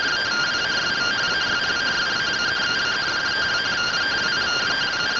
Two-tone teleprinter signal, about 170 Hz shift.